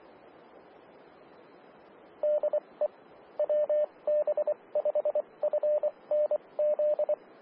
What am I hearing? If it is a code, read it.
Morse code, heard as a keyed tone, about 24 wpm, spaced as for 20 wpm, 600 Hz, with radio noise. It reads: DE WB5FNZ
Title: Austin, TX Amateur Repeaters